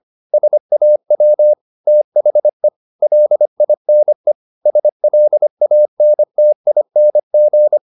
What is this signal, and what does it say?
Morse code, heard as a keyed tone, about 25 wpm, 600 Hz. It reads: SAW THE LINE SLANTING
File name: the old man and the sea part 1 25wpm